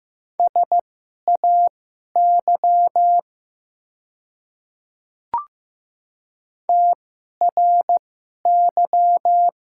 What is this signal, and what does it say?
Morse code, heard as a keyed tone, about 15 wpm, spaced as for 11 wpm, 700 Hz. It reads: SAY TRY
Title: 2x Character Spacing - Top 200 Words - 15wpm